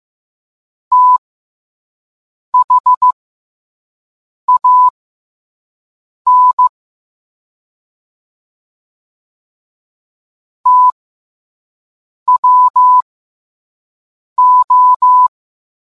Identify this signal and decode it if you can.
Morse code, heard as a keyed tone, about 15 wpm, spaced as for 5 wpm, 1000 Hz. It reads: THAN TWO